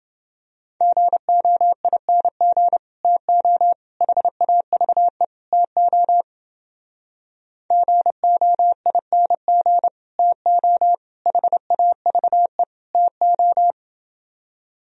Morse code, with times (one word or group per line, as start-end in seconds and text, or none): 0.8-2.8 GOING
3.0-3.7 TO
4.0-5.2 HAVE
5.5-6.2 TO
7.7-9.9 GOING
10.2-10.9 TO
11.3-12.6 HAVE
12.9-13.7 TO